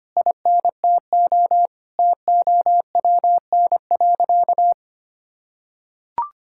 Morse code, heard as a keyed tone, about 25 wpm, 700 Hz. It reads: INTO TOWN.
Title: Sentences from Top 1000 Words - Review Part 6 - 25wpm